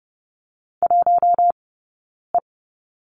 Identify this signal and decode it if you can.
Morse code, heard as a keyed tone, about 30 wpm, spaced as for 9 wpm, 690 Hz. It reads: 1E
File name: Lesson 17d - Letter u - Two Characters 30wpm